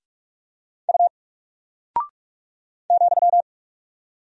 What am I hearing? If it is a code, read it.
Morse code, heard as a keyed tone, about 45 wpm, 700 Hz. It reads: U ,
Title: Single Character (Letter:Number:Punctuation:Prosign) Alt 45wpm